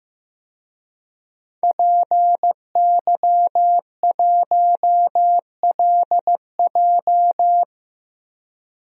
Morse code, heard as a keyed tone, about 15 wpm, 700 Hz. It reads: PY1LJ
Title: DXCC-100 Call Signs - 5 Characters - Repeated 2x - 15wpm